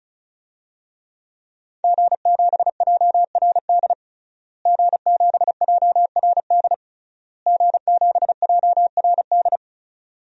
Morse code, heard as a keyed tone, about 34 wpm, 700 Hz. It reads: G7JRD G7JRD G7JRD
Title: DXCC-100 Call Signs - 5 Characters - Repeated 3x - 35wpm